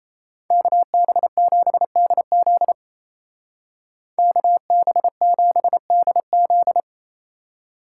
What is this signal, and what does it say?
Morse code, heard as a keyed tone, about 34 wpm, 700 Hz. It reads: KB7DZ KB7DZ
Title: Speed Racing - USA 2x2 Call Signs 28wpm